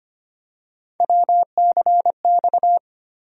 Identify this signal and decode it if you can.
Morse code, heard as a keyed tone, about 25 wpm, 700 Hz. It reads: WCX
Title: USA Call Sign Postfixes - 3 Letters 25wpm